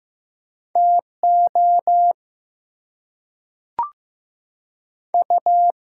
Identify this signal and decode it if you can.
Morse code, heard as a keyed tone, about 15 wpm, 700 Hz. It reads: TO U
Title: Two Letter Words - 15wpm